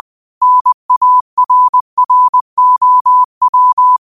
Morse code, heard as a keyed tone, about 20 wpm, 1000 Hz. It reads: NARROW
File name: qotdfiles / 20wpm